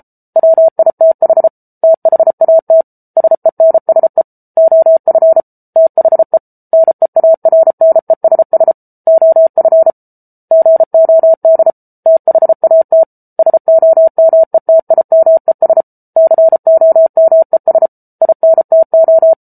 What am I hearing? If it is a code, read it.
Morse code, heard as a keyed tone, about 34 wpm, 650 Hz. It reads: WITH THAT SENSE OF THE NEARNESS OF GOD THAT SOMETIMES COMES INTO